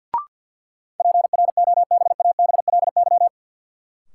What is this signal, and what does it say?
Morse code, heard as a keyed tone, about 51 wpm, 700 Hz. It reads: PROBABLY
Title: Sentences from Top 700 Words - New Word Drill - 50wpm